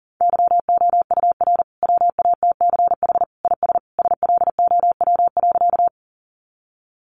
Morse code, heard as a keyed tone, about 40 wpm, 700 Hz. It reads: YOUR WATCH IS SLOW.
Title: Sentences from Top 400 Words 40wpm